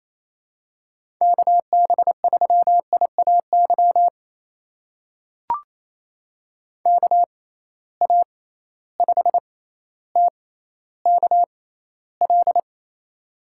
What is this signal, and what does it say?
Morse code, heard as a keyed tone, about 28 wpm, spaced as for 14 wpm, 700 Hz. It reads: KB3IAY KA5TKL
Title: USA Call Signs - 6 Characters - 6x Character Spacing - 28wpm